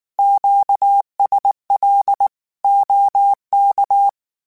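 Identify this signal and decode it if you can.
Morse code, heard as a keyed tone, about 19 wpm, 800 Hz. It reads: QSL OK
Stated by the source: Morse code practice text at 18 wpm, from Portal do CW.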